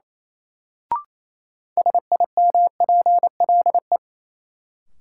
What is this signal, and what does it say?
Morse code, heard as a keyed tone, about 28 wpm, 700 Hz. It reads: SIMPLE
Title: Sentences from Top 400 Words - New Word Drill - 28wpm